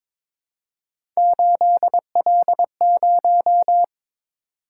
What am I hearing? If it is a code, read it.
Morse code, heard as a keyed tone, about 22 wpm, 700 Hz. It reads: 8L0
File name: practice / 22wpm / Three Letter-Number Combinations - 22wpm